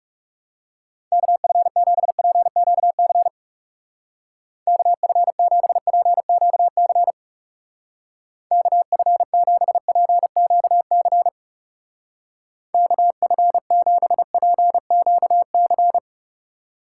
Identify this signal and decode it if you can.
Morse code, heard as a keyed tone, about 37 wpm, 700 Hz. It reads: KF7PQC KF7PQC KF7PQC KF7PQC
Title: Speed Racing - USA 2x3 Call Signs 30wpm